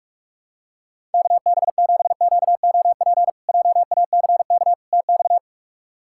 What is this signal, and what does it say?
Morse code, heard as a keyed tone, about 44 wpm, 700 Hz. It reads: KB7QOP JACK TX
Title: K1USN SST QSO Practice - 45wpm